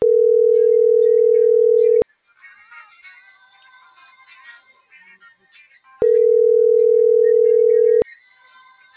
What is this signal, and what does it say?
Ringback tone (North America), 440 + 480 Hz.